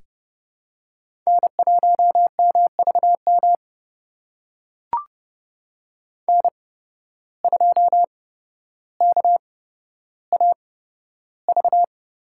Morse code, heard as a keyed tone, about 31 wpm, spaced as for 13 wpm, 700 Hz. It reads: N1MVM N2KAV
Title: USA Call Signs - 5 Characters - 8x Character Spacing - 30wpm